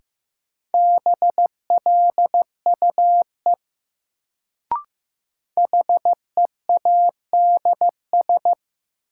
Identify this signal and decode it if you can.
Morse code, heard as a keyed tone, about 15 wpm, 700 Hz. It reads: BLUE HEADS
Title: Sentences from Top 1000 Words - Review Part 3 - New Word Drill - 15wpm